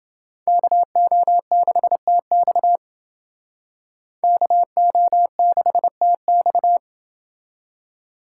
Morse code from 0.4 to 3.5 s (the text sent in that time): KO6TX